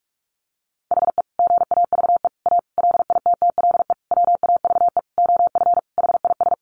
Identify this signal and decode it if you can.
Morse code, heard as a keyed tone, about 44 wpm, 700 Hz. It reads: HE GAVE A LITTLE WAVE OF HIS
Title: Sentences from Top 1400 Words 45wpm